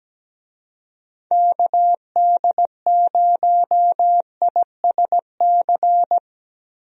Morse code, heard as a keyed tone, about 17 wpm, 700 Hz. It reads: KD0ISC
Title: USA Call Signs - 6 Characters - Repeated 3x - 17wpm